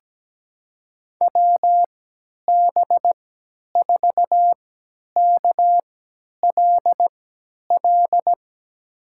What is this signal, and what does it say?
Morse code, heard as a keyed tone, about 17 wpm, spaced as for 10 wpm, 700 Hz. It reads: WB4KLL
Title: USA Call Signs - 6 Characters - 3x Character Spacing - 17wpm